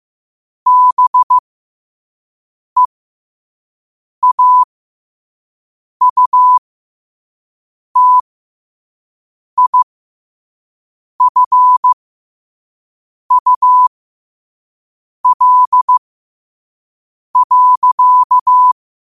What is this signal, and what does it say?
Morse code, heard as a keyed tone, about 15 wpm, spaced as for 5 wpm, 1000 Hz. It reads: BEAUTIFUL.
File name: qotdfiles / 5wpm